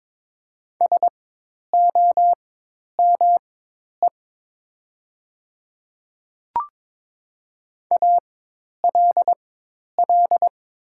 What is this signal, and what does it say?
Morse code, heard as a keyed tone, about 22 wpm, spaced as for 10 wpm, 700 Hz. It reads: SOME ALL
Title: 4x Character Spacing - Top 100 Words - 22wpm